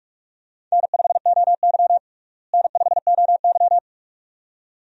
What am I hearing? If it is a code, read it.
Morse code, heard as a keyed tone, about 43 wpm, 700 Hz. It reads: N5OY N5OY
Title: USA Call Signs - 4 Characters - Repeated 2x - 45wpm